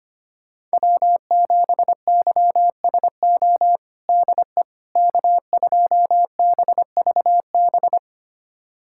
Morse code, heard as a keyed tone, about 25 wpm, 700 Hz. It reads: W7YSO DE K2BVB
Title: USA Callsign DE Callsign - 5 Characters - 25wpm